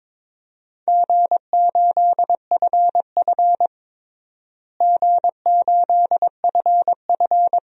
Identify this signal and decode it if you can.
Morse code, heard as a keyed tone, about 22 wpm, 700 Hz. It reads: G8FF G8FF
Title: DXCC-100 Call Signs - 4 Characters - Repeated 2x - 22wpm